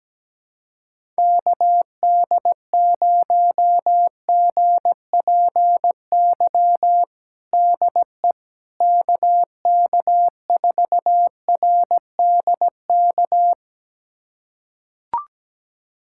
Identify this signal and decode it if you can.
Morse code, heard as a keyed tone, about 17 wpm, 700 Hz. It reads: KD0GPY DE KK4RDK
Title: USA Callsign DE Callsign 17wpm